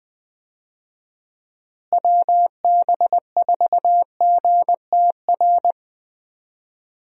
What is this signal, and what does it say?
Morse code, heard as a keyed tone, about 20 wpm, 700 Hz. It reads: WB4GTR
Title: USA Call Signs - 6 Characters - Repeated 3x - 20wpm